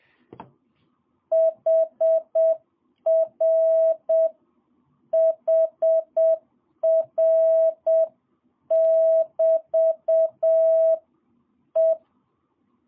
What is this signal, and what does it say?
Morse code, heard as a keyed tone, about 7 wpm, 650 Hz. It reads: HR HR <BT> E
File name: jh8cbh / 2022cw